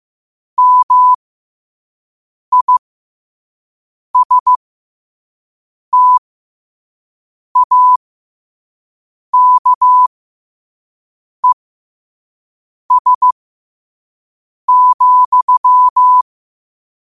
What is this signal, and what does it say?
Morse code, heard as a keyed tone, about 15 wpm, spaced as for 5 wpm, 1000 Hz. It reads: MISTAKES,